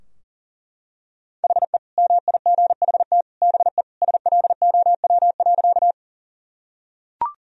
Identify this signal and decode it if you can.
Morse code, heard as a keyed tone, about 37 wpm, 700 Hz. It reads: HE MIGHT BE SLOW.
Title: Sentences from Top 1000 Words - Review Part 2 - 40wpm